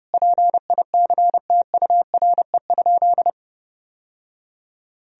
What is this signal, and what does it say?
Morse code, heard as a keyed tone, about 32 wpm, 700 Hz. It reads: PICTURE?
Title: Sentences from Top 1000 Words - Review Part 2 - 30wpm